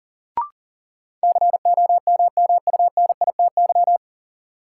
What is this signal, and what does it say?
Morse code, heard as a keyed tone, about 38 wpm, 700 Hz. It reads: COMMUNITY
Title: Sentences from Top 1200 Words - New Word Drill Alt - 40wpm